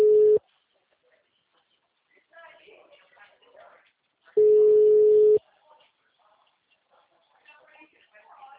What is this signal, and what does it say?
Ringback tone (Europe, 425 Hz), with speech.
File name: out- -2005- - - .53